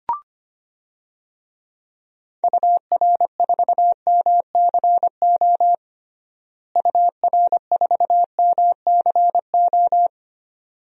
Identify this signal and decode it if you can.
Morse code, heard as a keyed tone, about 25 wpm, 700 Hz. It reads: UR4MCO UR4MCO
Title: DXCC-100 Call Signs - 6 Characters - Repeated 2x - 25wpm